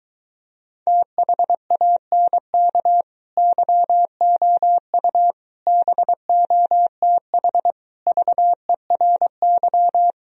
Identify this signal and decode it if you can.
Morse code, heard as a keyed tone, about 24 wpm, 700 Hz. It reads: THANK YOU BOTH VERY